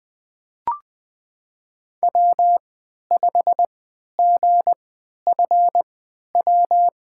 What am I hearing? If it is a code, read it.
Morse code, heard as a keyed tone, about 20 wpm, spaced as for 11 wpm, 700 Hz. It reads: W5GFW
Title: USA Call Signs - 5 Characters - 3x Character Spacing - 20wpm